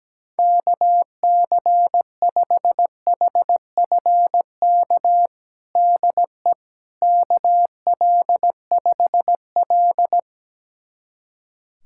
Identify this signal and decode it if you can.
Morse code, heard as a keyed tone, about 17 wpm, 700 Hz. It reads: KC5HFK DE KL5L
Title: USA Callsign DE Callsign 17wpm